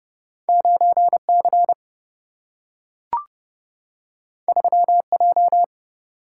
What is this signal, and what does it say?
Morse code, heard as a keyed tone, about 31 wpm, 700 Hz. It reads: 9C 3J
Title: Single-Letter Single-Number 30wpm